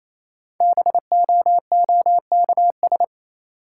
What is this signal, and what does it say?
Morse code, heard as a keyed tone, about 28 wpm, 700 Hz. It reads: BOOKS
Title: Sentences from Top 1000 Words - Review Part 4 - New Word Drill - 28wpm